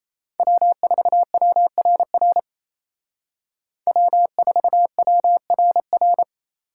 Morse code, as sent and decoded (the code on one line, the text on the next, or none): .-- ....- .-- .-. .-. / .-- ....- .-- .-. .-.
W4WRR W4WRR